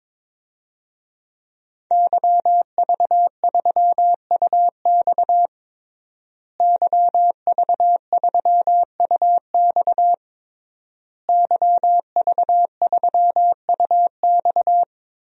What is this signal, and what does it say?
Morse code, heard as a keyed tone, about 22 wpm, 700 Hz. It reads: YV3UX YV3UX YV3UX
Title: DXCC-100 Call Signs - 5 Characters - Repeated 3x - 22wpm